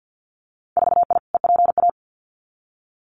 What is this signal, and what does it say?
Morse code, heard as a keyed tone, about 50 wpm, 700 Hz. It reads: 4I EPA
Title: Winter Field Day Exchanges 50wpm